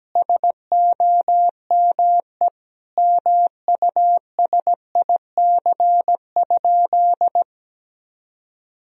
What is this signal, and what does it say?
Morse code, heard as a keyed tone, about 17 wpm, 700 Hz. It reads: SOME MUSIC?
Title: Sentences from Top 1000 Words - Review Part 1 - 17wpm